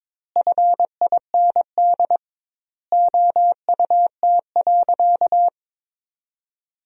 Morse code, heard as a keyed tone, about 22 wpm, 700 Hz. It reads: FIND OUT.
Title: 2x Word-Spacing - Sentences from Top 200 Words - 22wpm